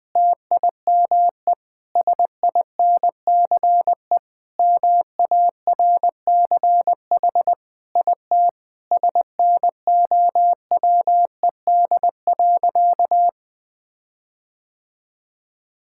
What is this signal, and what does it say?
Morse code, heard as a keyed tone, about 20 wpm, 700 Hz. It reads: TIME SINCE MARCH IT SNOWED.
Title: Sentences with 5-Word Phrases - 20wpm